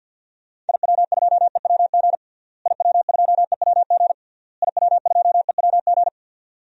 Morse code, heard as a keyed tone, about 53 wpm, 700 Hz. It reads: IW2EWG IW2EWG IW2EWG
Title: DXCC-100 Call Signs - 6 Characters - Repeated 3x - 50wpm